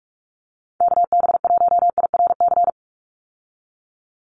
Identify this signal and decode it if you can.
Morse code, heard as a keyed tone, about 46 wpm, 700 Hz. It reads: KB1IRC
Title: USA Call Signs 45wpm